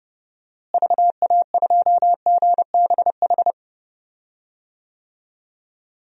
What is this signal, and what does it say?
Morse code, heard as a keyed tone, about 32 wpm, 700 Hz. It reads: VA2GBH
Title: DXCC-100 Call Signs - 6 Characters - Repeated 3x - 30wpm